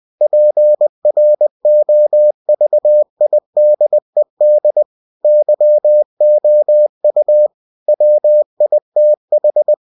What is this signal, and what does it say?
Morse code, heard as a keyed tone, about 20 wpm, 600 Hz. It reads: PROVIDED YOU WITH
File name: cw-books / winnie-the-pooh / winnie-20wpm-page-0119